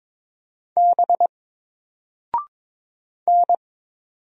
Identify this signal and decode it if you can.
Morse code, heard as a keyed tone, about 22 wpm, 700 Hz. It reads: B N
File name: Single-Letter-Number-22wpm